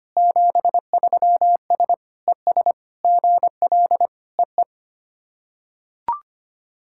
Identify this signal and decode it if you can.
Morse code, heard as a keyed tone, about 25 wpm, 690 Hz. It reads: 73S ES GL EE
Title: SOTA QSO Practice 25wpm